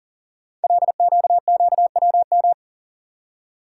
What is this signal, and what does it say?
Morse code, heard as a keyed tone, about 40 wpm, 700 Hz. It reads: LQQWM
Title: Code Groups - 5 Letters - 40wpm